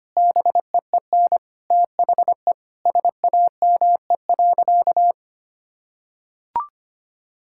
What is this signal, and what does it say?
Morse code, heard as a keyed tone, about 25 wpm, 700 Hz. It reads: BEEN THE SAME.
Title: Sentences from Top 1000 Words - Review Part 6 - 25wpm